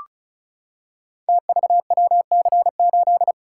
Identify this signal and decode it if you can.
Morse code, heard as a keyed tone, about 36 wpm, 700 Hz. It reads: TVWC8
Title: Code Groups - 5 Letter:Numbers - 35wpm